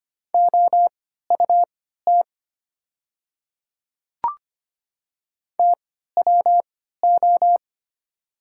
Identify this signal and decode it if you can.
Morse code, heard as a keyed tone, about 25 wpm, spaced as for 14 wpm, 700 Hz. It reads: OUT TWO
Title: 3x Character Spacing - Top 100 Words - 25wpm